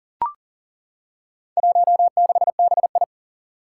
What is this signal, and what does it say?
Morse code, heard as a keyed tone, about 40 wpm, 700 Hz. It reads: 16BI
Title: Four Challenging Letter-Number Combinations 40wpm